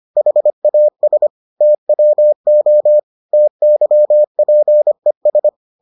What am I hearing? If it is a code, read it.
Morse code, heard as a keyed tone, about 25 wpm, 600 Hz. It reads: HAS TWO TYPES